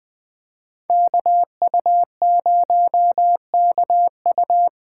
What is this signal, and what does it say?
Morse code, heard as a keyed tone, about 24 wpm, 700 Hz. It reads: KU0KU
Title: Speed Racing - USA 2x2 Call Signs 20wpm